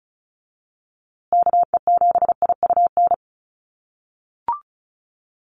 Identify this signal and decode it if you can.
Morse code, heard as a keyed tone, about 34 wpm, 700 Hz. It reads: KE7IUN
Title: USA Call Signs Repeated 3x 35wpm